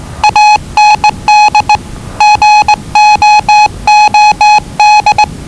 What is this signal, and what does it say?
Morse code, heard as a keyed tone, about 20 wpm, 840 Hz. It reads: AND GOOD
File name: WellingtonRadioZlwCloseDownOnHfAndMf / 01-Zlw 4277khz 417.5 qrt until 0925z